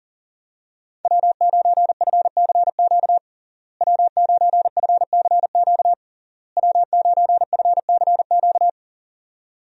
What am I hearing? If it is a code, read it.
Morse code, heard as a keyed tone, about 37 wpm, 700 Hz. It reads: W9FCQ W9FCQ W9FCQ
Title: USA Call Signs - 5 Characters - Repeated 3x - 40wpm